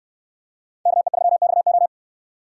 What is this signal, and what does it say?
Morse code, heard as a keyed tone, about 67 wpm, 700 Hz. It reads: KE3XO